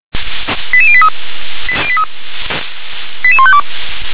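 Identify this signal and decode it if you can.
Five-tone selective call (ZVEI), heard as quick runs of five tones, tones of about 70 ms. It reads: 88083 88152
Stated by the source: SDR-received ZVEI five-tone selective call.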